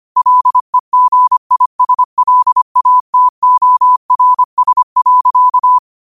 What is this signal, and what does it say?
Morse code, heard as a keyed tone, about 25 wpm, 1000 Hz. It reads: LEGISLATORS.
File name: qotdfiles / 25wpm